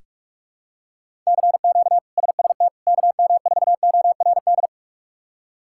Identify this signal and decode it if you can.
Morse code, heard as a keyed tone, about 43 wpm, 700 Hz. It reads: CQ SST KM4ORD
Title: K1USN SST QSO Practice - 45wpm